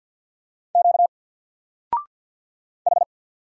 Morse code, heard as a keyed tone, about 48 wpm, 700 Hz. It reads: Q H
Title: Single Letter-Number Alt 50wpm